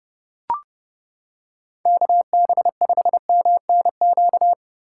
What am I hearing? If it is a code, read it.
Morse code, heard as a keyed tone, about 29 wpm, 700 Hz. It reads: KB5MNQ